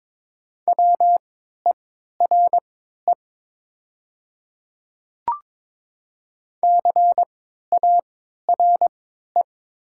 Morse code, heard as a keyed tone, about 22 wpm, spaced as for 12 wpm, 700 Hz. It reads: WERE CARE
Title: 3x Character Spacing - Top 300 Words - 22wpm